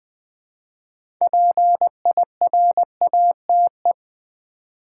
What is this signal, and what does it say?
Morse code, heard as a keyed tone, about 20 wpm, 700 Hz. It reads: PIRATE
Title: 6 Letter Words - First 3 Letters Given - 20wpm